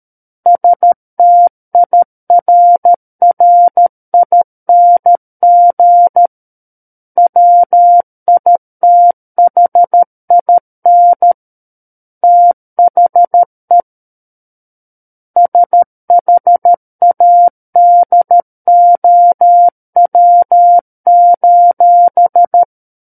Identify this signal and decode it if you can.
Morse code, heard as a keyed tone, about 13 wpm, 700 Hz. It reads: STIRRING WITHIN THE SHADOW:
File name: Book One / 04-Chapter 4 - 13 WPM